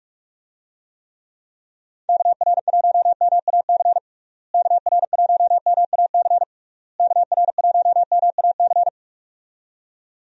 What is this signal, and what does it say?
Morse code, heard as a keyed tone, about 46 wpm, 700 Hz. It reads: KR1MAC KR1MAC KR1MAC
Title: USA Call Signs - 6 Characters - Repeated 3x - 45wpm